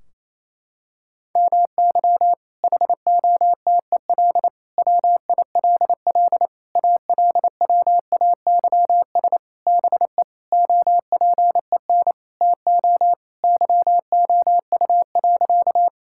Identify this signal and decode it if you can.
Morse code, heard as a keyed tone, about 28 wpm, 700 Hz. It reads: MY HOTEL WILL ALWAYS BE OPEN TO YOU.
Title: Sentences from Top 1600 Words 28wpm